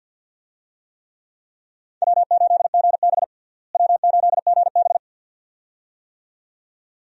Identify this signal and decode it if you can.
Morse code, heard as a keyed tone, about 46 wpm, 700 Hz. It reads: W8GB W8GB
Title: USA Call Signs - 4 Characters - Repeated 2x - 50wpm